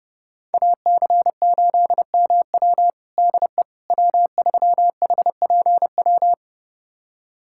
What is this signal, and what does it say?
Morse code, heard as a keyed tone, about 29 wpm, 700 Hz. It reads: AC8MW DE W3HPW
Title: USA Callsign DE Callsign - 5 Characters - 30wpm